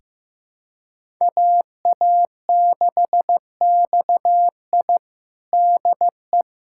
Morse code, heard as a keyed tone, about 15 wpm, 700 Hz. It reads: AA6XI DE
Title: USA Callsign DE Callsign - 5 Characters - 15wpm